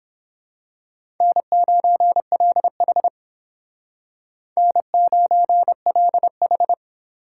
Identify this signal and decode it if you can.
Morse code, heard as a keyed tone, about 26 wpm, 700 Hz. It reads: N9LH N9LH
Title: Speed Racing - USA 1x2 Call Signs 22wpm